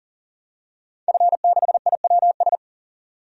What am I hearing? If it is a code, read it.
Morse code, heard as a keyed tone, about 40 wpm, 700 Hz. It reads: F6IWS
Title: DXCC-100 Call Signs - 5 Characters - 40wpm